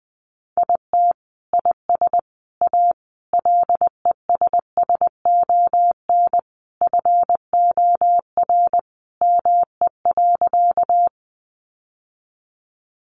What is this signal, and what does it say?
Morse code, heard as a keyed tone, about 20 wpm, 700 Hz. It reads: IT IS A LESSON FOR ME.
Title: Sentences from Top 1200 Words 20wpm